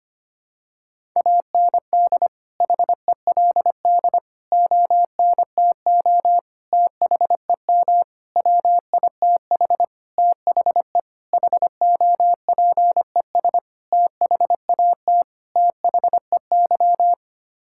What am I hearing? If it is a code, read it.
Morse code, heard as a keyed tone, about 25 wpm, 700 Hz. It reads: AND HELD ONTO THEM WITH THE HOPES THAT THEY
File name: Sets of 9 Words - 25wpm